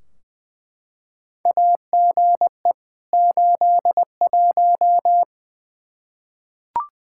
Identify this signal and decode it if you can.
Morse code, heard as a keyed tone, about 20 wpm, 700 Hz. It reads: AGE 81
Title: Practice QSO - Exchange 3 - More Info - 20wpm